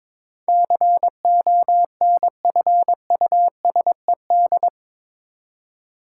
Morse code, heard as a keyed tone, about 22 wpm, 700 Hz. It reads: CONFUSED
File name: practice / 22wpm / Eight Letter Words - 22wpm